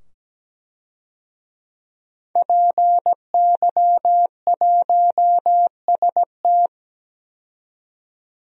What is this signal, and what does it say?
Morse code, heard as a keyed tone, about 17 wpm, 700 Hz. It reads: PY1ST
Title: DXCC-100 Call Signs - 5 Characters - 17wpm